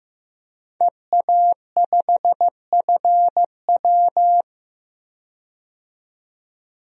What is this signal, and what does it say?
Morse code, heard as a keyed tone, about 16 wpm, 700 Hz. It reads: EA5FW
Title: DXCC-100 Call Signs - 5 Characters - 15wpm